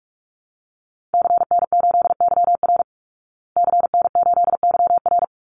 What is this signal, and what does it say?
Morse code, heard as a keyed tone, about 44 wpm, 700 Hz. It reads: CN8YR CN8YR
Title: Call Signs Repeated 3x - 45wpm